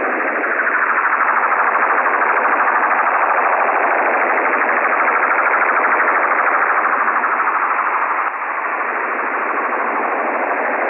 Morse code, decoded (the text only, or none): none